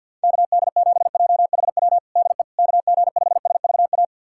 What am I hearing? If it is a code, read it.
Morse code, heard as a keyed tone, about 51 wpm, 700 Hz. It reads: KD7JHW DE KG5SVA
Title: USA Callsign DE Callsign - 6 Characters - 50wpm